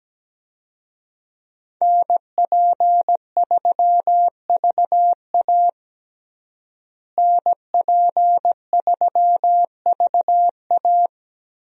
Morse code, heard as a keyed tone, about 17 wpm, 700 Hz. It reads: NP3VA NP3VA
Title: USA Call Signs - 5 Characters - Repeated 2x - 17wpm